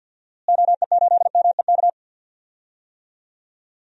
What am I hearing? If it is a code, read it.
Morse code, heard as a keyed tone, about 47 wpm, 700 Hz. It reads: OE8MEK
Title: DXCC-100 Call Signs - 6 Characters - Repeated 2x - 50wpm